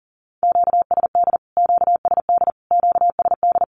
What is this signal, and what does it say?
Morse code, heard as a keyed tone, about 40 wpm, 700 Hz. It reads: QSD QSD QSD
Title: Q-Codes Repeated 3x 40wpm